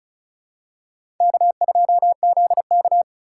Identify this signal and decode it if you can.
Morse code, heard as a keyed tone, about 34 wpm, 700 Hz. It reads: K2ZK
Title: USA Call Signs - 4 Characters - 2x Character Spacing - 35wpm